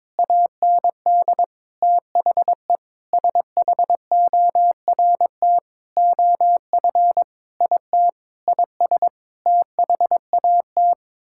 Morse code, heard as a keyed tone, about 22 wpm, 700 Hz. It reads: AND THE SHORT OF IT IS THAT
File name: Sets of 9 Words - 22wpm